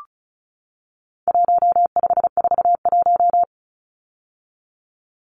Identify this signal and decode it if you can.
Morse code, heard as a keyed tone, about 34 wpm, 700 Hz. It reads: 1541
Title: Numbers - 4-Digit 35wpm